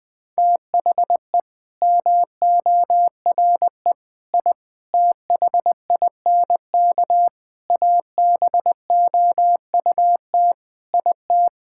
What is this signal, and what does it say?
Morse code, heard as a keyed tone, about 20 wpm, 700 Hz. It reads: THE MORE I THINK ABOUT IT
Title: Sentences with 8-Word Phrases - 20wpm